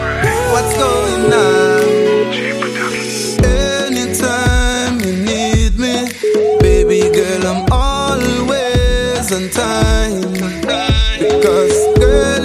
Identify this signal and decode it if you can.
Ringback tone (Europe, 425 Hz), with noise.